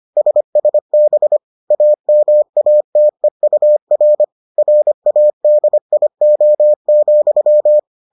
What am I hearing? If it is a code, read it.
Morse code, heard as a keyed tone, about 25 wpm, 600 Hz. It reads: SSB AMATEUR RADIO,